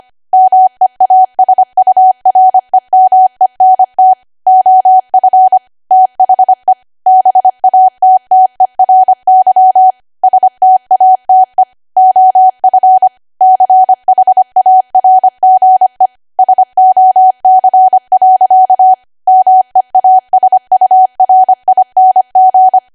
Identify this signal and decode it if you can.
Morse code, heard as a keyed tone, about 25 wpm, 750 Hz. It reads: MEASUREMENT OF THE BATTERY STATE OF CHARGE SOC. MEASURING
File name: Archive / 25 WPM / 25WPM